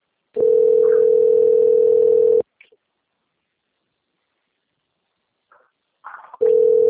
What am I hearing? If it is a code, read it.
Ringback tone (North America), 440 + 480 Hz.
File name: rg-1600-unknown- - - .8290